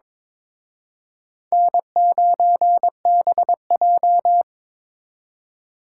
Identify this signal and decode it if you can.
Morse code, heard as a keyed tone, about 26 wpm, 700 Hz. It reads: N9BJ
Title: Speed Racing - USA 1x2 Call Signs 22wpm